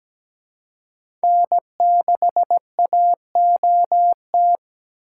Morse code, heard as a keyed tone, about 17 wpm, 700 Hz. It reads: N6AOT
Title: USA Call Signs - 5 Characters - 4x Character Spacing - 17wpm